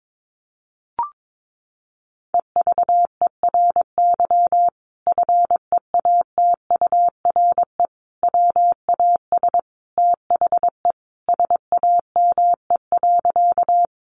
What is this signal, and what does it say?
Morse code, heard as a keyed tone, about 22 wpm, 700 Hz. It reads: EVERY FEATURE WAS THE SAME.
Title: Sentences from Top 800 Words 22wpm